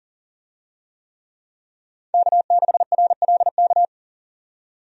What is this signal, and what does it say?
Morse code, heard as a keyed tone, about 40 wpm, 700 Hz. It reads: K6RLK
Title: USA Call Signs - 5 Characters - 40wpm